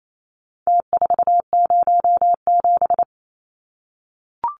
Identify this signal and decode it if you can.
Morse code, heard as a keyed tone, about 28 wpm, 700 Hz. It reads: T407
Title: Numbers - Contest Style - 4-Digit 28wpm